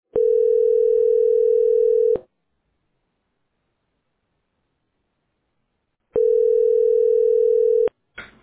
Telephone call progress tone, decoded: ringback tone